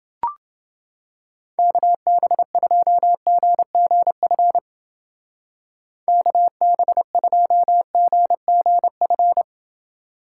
Morse code, heard as a keyed tone, about 30 wpm, 700 Hz. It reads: KB2GGF KB2GGF